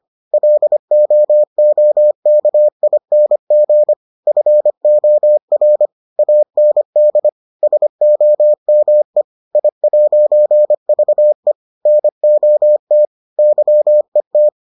Morse code, heard as a keyed tone, about 25 wpm, 600 Hz. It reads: LOOKING FOR AND SOME I'VE NOT YET